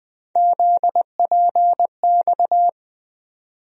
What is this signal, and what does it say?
Morse code, heard as a keyed tone, about 20 wpm, 700 Hz. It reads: ZPX
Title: USA Call Sign Postfixes - 3 Letters 20wpm